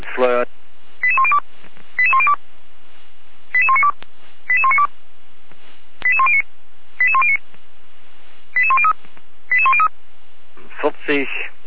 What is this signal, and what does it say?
Five-tone selective call (ZVEI), heard as quick runs of five tones, tones of about 70 ms, with speech. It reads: 88102 88102 88192 88192 88109 88109 88103 88103